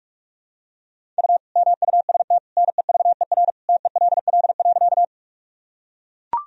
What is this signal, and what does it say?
Morse code, heard as a keyed tone, about 41 wpm, 700 Hz. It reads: U MUST NEVER TELL.
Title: Sentences from Top 300 Words and Abbreviations 45wpm